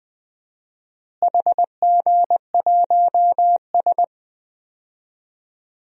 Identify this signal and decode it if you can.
Morse code, heard as a keyed tone, about 20 wpm, 700 Hz. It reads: HG1S
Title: DXCC-100 Call Signs - 4 Characters - Repeated 2x - 20wpm